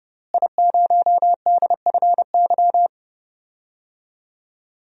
Morse code, heard as a keyed tone, about 30 wpm, 700 Hz. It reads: I0DFY
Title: DXCC-100 Call Signs - 5 Characters - Repeated 3x - 30wpm